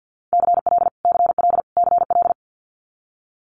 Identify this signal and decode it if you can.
Morse code, heard as a keyed tone, about 50 wpm, 700 Hz. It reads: CL CL CL
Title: Multi-letter Phonemes Repeated 3x 50wpm